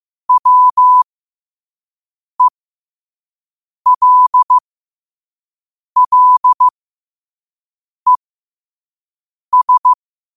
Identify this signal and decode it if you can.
Morse code, heard as a keyed tone, about 15 wpm, spaced as for 5 wpm, 1000 Hz. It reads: WELLES
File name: 5wpm